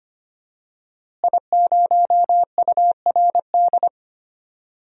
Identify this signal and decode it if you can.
Morse code, heard as a keyed tone, about 25 wpm, 700 Hz. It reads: I0URD
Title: DXCC-100 Call Signs - 5 Characters - Repeated 3x - 25wpm